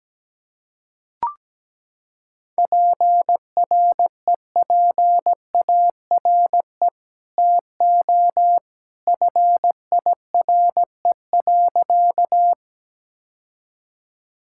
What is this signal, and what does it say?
Morse code, heard as a keyed tone, about 17 wpm, 700 Hz. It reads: PREPARE TO FIRE.
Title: Sentences from Top 1000 Words - Review Part 6 - 17wpm